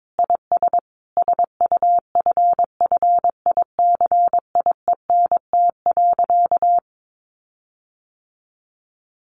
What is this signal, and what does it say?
Morse code, heard as a keyed tone, about 22 wpm, 700 Hz. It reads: IS SUFFICIENT.
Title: Sentences from Top 1400 Words 22wpm